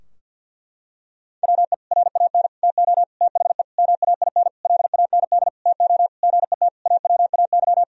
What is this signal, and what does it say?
Morse code, heard as a keyed tone, about 50 wpm, 700 Hz. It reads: WE RAN TO THE MAINLAND TO GET AWAY
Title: Sentences from Top 1000 Words - Review Part 2 - 50wpm